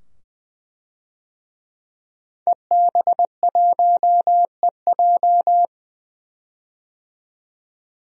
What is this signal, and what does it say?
Morse code, heard as a keyed tone, about 20 wpm, 700 Hz. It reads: EB1EJ
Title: DXCC-100 Call Signs - 5 Characters - 20wpm